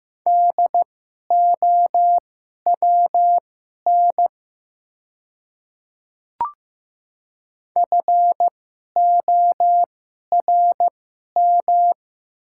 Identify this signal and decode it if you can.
Morse code, heard as a keyed tone, about 15 wpm, spaced as for 11 wpm, 690 Hz. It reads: DOWN FORM
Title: 2x Character Spacing - Top 300 Words - 15wpm